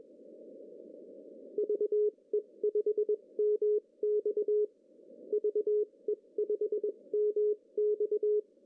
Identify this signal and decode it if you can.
Morse code, heard as a keyed tone, about 22 wpm, spaced as for 18 wpm, 410 Hz. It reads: VE5MX VE5MX
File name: CW / cqww-cw-k5tr